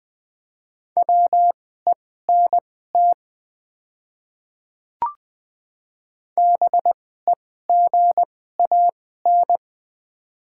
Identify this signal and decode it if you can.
Morse code, heard as a keyed tone, about 20 wpm, spaced as for 14 wpm, 700 Hz. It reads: WENT BEGAN
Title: 2x Character Spacing - Top 300 Words - 20wpm